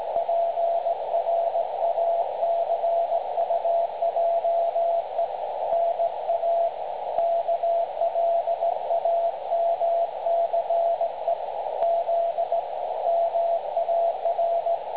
Morse code, heard as a keyed tone, about 17 wpm, 680 Hz, with radio noise. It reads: PRACUJE NA KLAMCE Z MAR
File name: dlazaawansowanych / nagraniazpasma / websdr recording 3565.0kHz